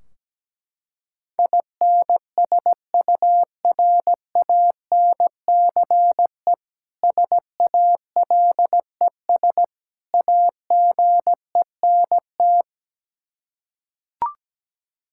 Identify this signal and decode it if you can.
Morse code, heard as a keyed tone, about 17 wpm, 700 Hz. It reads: INSURANCE SALES AGENT
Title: K1USN SST QSO Practice 17wpm